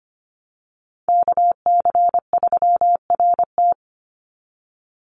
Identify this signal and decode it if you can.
Morse code, heard as a keyed tone, about 25 wpm, 700 Hz. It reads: KC3RT
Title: NA SOTA Chaser Callsigns 25wpm